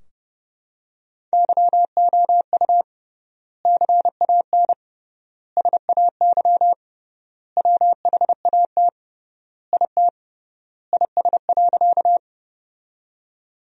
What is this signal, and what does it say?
Morse code, heard as a keyed tone, about 32 wpm, 700 Hz. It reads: YOU CAN SAY WHAT IT IS.
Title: 3x Word-Spacing - Sentences from Top 200 Words - 30wpm